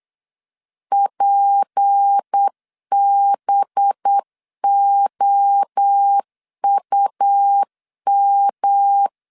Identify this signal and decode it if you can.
Morse code, heard as a keyed tone, about 8 wpm, 800 Hz. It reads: PBOUM
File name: LEZIONI / 6 PKX0 / 6 P 8wpm 3025